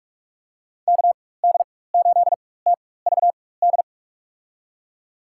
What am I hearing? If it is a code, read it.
Morse code, heard as a keyed tone, about 46 wpm, spaced as for 21 wpm, 700 Hz. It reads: KD8TVD
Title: USA Call Signs - 4x Character Spacing - 45wpm